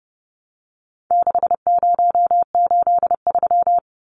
Morse code, heard as a keyed tone, about 29 wpm, 700 Hz. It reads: 6083
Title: Numbers - 4-Digit 30wpm